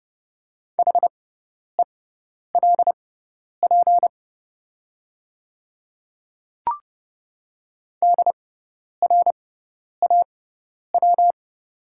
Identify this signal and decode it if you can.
Morse code, heard as a keyed tone, about 30 wpm, spaced as for 10 wpm, 700 Hz. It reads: HELP DRAW
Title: 6x Character Spacing - Top 300 Words - 30wpm